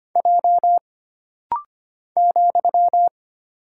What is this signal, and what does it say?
Morse code, heard as a keyed tone, about 25 wpm, 700 Hz. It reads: J ,
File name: Single Character (Letter:Number:Punctuation:Prosign) - Rapid-Fire - 25wpm